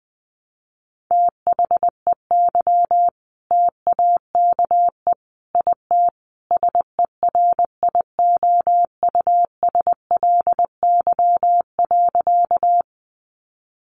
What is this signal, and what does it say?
Morse code, heard as a keyed tone, about 20 wpm, 700 Hz. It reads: THEY TAKE IT SERIOUSLY.
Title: Sentences from Top 1200 Words 20wpm